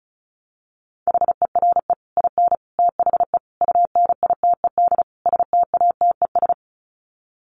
Morse code, heard as a keyed tone, about 34 wpm, 700 Hz. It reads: HERE IN THE UNITED STATES
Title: Sets of 5 Words 35wpm